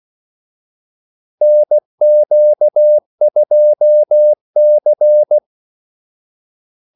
Morse code, heard as a keyed tone, about 16 wpm, 600 Hz. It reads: NQ2C
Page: Kaldesignaler 16 wpm